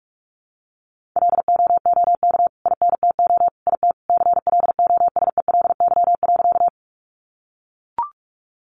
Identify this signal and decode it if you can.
Morse code, heard as a keyed tone, about 44 wpm, 700 Hz. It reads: LOOK INTO IT CLOSELY.
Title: Sentences from Top 1400 Words 45wpm